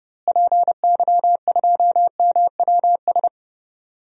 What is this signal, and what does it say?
Morse code, heard as a keyed tone, about 30 wpm, 700 Hz. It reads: PY2MWS
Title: DXCC-100 Call Signs - 6 Characters - Repeated 2x - 30wpm